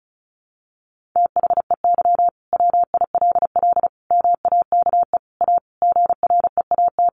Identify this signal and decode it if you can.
Morse code, heard as a keyed tone, about 34 wpm, 700 Hz. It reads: THEY WILL MAKE A GREAT
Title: Sentences from Top 1500 Words 35wpm